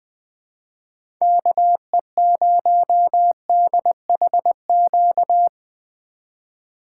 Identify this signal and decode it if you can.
Morse code, heard as a keyed tone, about 20 wpm, 700 Hz. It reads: KE0DHQ
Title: USA Call Signs - 6 Characters - 3x Character Spacing - 20wpm